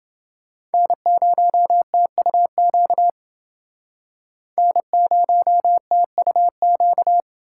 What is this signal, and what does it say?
Morse code, heard as a keyed tone, about 29 wpm, 700 Hz. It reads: N0TUQ N0TUQ